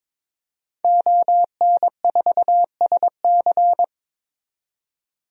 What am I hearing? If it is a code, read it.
Morse code, heard as a keyed tone, about 22 wpm, 700 Hz. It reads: ON4SC
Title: DXCC-100 Call Signs - 5 Characters - 22wpm